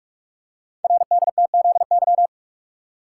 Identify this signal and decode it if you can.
Morse code, heard as a keyed tone, about 44 wpm, 690 Hz. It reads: RDTZY
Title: Code Groups - 5 Letters - 45wpm